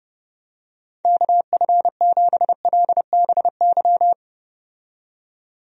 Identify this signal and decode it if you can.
Morse code, heard as a keyed tone, about 30 wpm, 700 Hz. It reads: KF7LBY
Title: USA Call Signs - 6 Characters - 30wpm